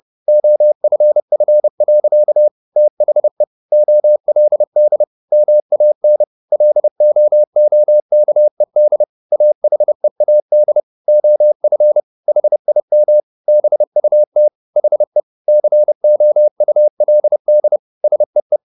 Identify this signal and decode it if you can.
Morse code, heard as a keyed tone, about 30 wpm, 600 Hz. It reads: OFF. THE OLD MAN LOOKED AHEAD OF HIM BUT HE COULD SEE